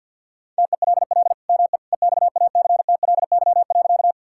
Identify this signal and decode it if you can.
Morse code, heard as a keyed tone, about 50 wpm, 700 Hz. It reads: TELL ME EXACTLY.
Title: Sentences from Top 1000 Words - Review Part 6 - 50wpm